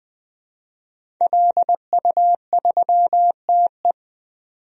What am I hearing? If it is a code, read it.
Morse code, heard as a keyed tone, about 20 wpm, 700 Hz. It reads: LU3TE
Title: DXCC-100 Call Signs - 5 Characters - 20wpm